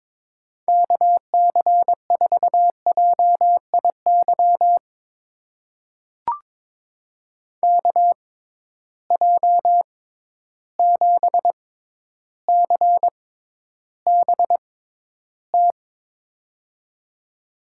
Morse code, heard as a keyed tone, about 22 wpm, spaced as for 11 wpm, 700 Hz. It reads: KC4JIY KJ7CBT
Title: USA Call Signs - 6 Characters - 6x Character Spacing - 22wpm